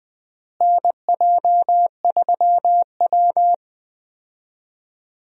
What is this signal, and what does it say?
Morse code, heard as a keyed tone, about 21 wpm, 700 Hz. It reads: NJ3W
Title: USA Call Signs - 4 Characters - Repeated 2x - 20wpm